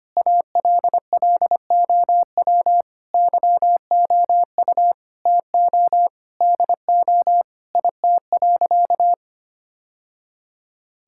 Morse code, heard as a keyed tone, about 25 wpm, 700 Hz. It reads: ALLOW YOU TO DO IT.
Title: Sentences from Top 1000 Words - Review Part 5 - 25wpm